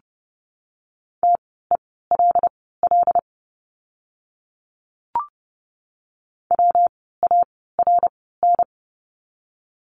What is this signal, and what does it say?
Morse code, heard as a keyed tone, about 30 wpm, spaced as for 17 wpm, 700 Hz. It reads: TELL WARN
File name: course / 30:10wpm / 115 Lesson 19f - Letter w - Words 30wpm